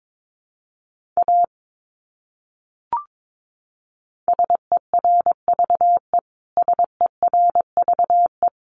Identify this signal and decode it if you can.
Morse code, heard as a keyed tone, about 22 wpm, 700 Hz. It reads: A SERVE SERVE
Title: Top 500 Words Repeated 22wpm